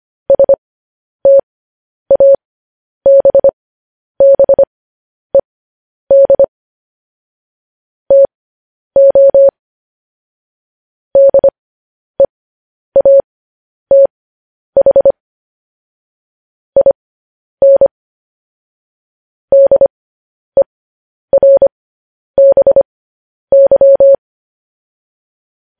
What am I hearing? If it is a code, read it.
Morse code, heard as a keyed tone, about 25 wpm, spaced as for 10 wpm, 550 Hz. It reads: STABBED TO DEATH IN DERBY
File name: podcast / News headlines-1x-25x10